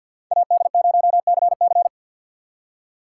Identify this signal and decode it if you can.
Morse code, heard as a keyed tone, about 64 wpm, 700 Hz. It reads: AD0CC